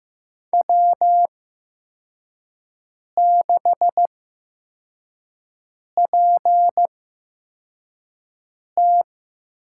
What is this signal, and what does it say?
Morse code, heard as a keyed tone, about 15 wpm, spaced as for 4 wpm, 700 Hz. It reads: W6PT
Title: USA Call Signs - 8x Character Spacing - 15wpm